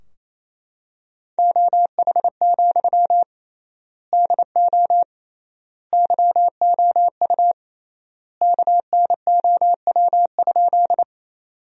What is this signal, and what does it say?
Morse code, heard as a keyed tone, about 28 wpm, 700 Hz. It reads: OH, DO YOU KNOW?
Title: 3x Word-Spacing - Sentences from Top 500 Words - 28wpm